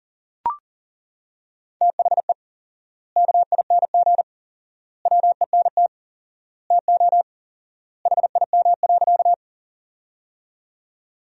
Morse code, heard as a keyed tone, about 40 wpm, 700 Hz. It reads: THE KING WENT TO HIM.
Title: 4x Word-Spacing - Sentences from Top 400 Words - 40wpm